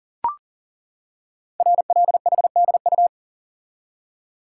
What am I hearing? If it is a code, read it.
Morse code, heard as a keyed tone, about 39 wpm, 700 Hz. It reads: RLHDU
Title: Code Groups - 5 Letters - 40wpm